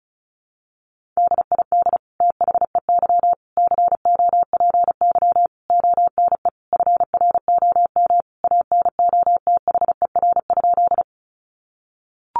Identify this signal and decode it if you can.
Morse code, heard as a keyed tone, about 34 wpm, 700 Hz. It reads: DID THEY COPY ONE FROM ANOTHER?
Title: Sentences from Top 1400 Words 35wpm